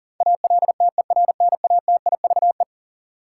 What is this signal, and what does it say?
Morse code, heard as a keyed tone, about 39 wpm, 700 Hz. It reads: ALTERNATIVE
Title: Sentences from Top 1000 Words - Review Part 4 - New Word Drill - 40wpm